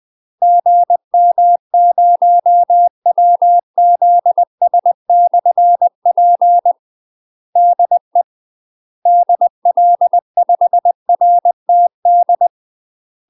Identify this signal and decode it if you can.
Morse code, heard as a keyed tone, about 20 wpm, 700 Hz. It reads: GM0WZS/P DE DL5RTD
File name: UK MASTER QSOs TRAINING/20WPM